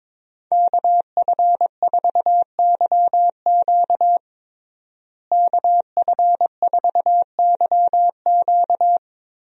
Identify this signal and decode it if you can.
Morse code, heard as a keyed tone, about 22 wpm, 700 Hz. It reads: KF4YQ KF4YQ
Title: USA Call Signs - 5 Characters - Repeated 2x - 22wpm